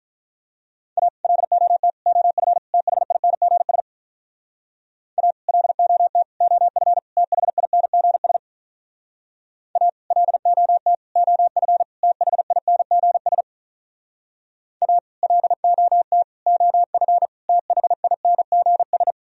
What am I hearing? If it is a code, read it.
Morse code, heard as a keyed tone, about 46 wpm, 700 Hz. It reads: A LOT OF THINGS A LOT OF THINGS A LOT OF THINGS A LOT OF THINGS